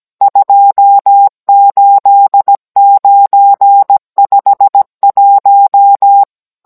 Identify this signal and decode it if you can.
Morse code, heard as a keyed tone, about 17 wpm, 800 Hz. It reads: 28951